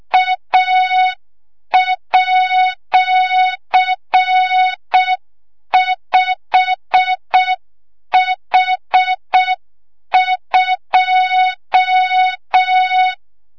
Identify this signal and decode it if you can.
Morse code, heard as a keyed tone, about 6 wpm, 3700 Hz. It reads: A@5H2
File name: Morze-HA2MN / hang / 25 GYAKORLAT